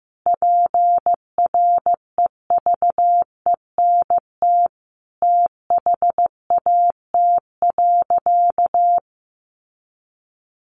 Morse code, heard as a keyed tone, about 15 wpm, 700 Hz. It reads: PREVENT THAT.
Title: Sentences from Top 1100 Words 15wpm